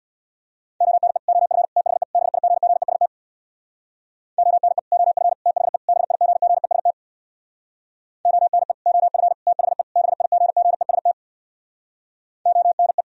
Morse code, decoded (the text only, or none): ONE OF THE BIGGEST ONE OF THE BIGGEST ONE OF THE BIGGEST ONE